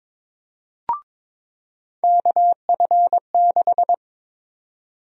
Morse code, heard as a keyed tone, about 22 wpm, 700 Hz. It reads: KF6
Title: USA Call Sign Prefixes - 22wpm